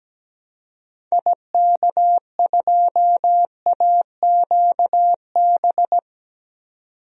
Morse code, heard as a keyed tone, about 17 wpm, 700 Hz. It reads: IK2AQB
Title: DXCC-100 Call Signs - 6 Characters - 17wpm